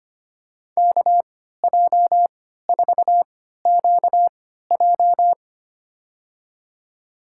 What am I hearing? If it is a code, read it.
Morse code, heard as a keyed tone, about 25 wpm, spaced as for 14 wpm, 700 Hz. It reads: KJ4QJ
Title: USA Call Signs - 5 Characters - 3x Character Spacing - 25wpm